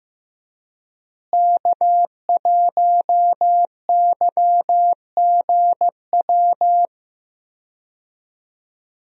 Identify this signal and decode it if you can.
Morse code, heard as a keyed tone, about 15 wpm, 700 Hz. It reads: K1YGW
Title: USA Call Signs - 5 Characters - Repeated 3x - 15wpm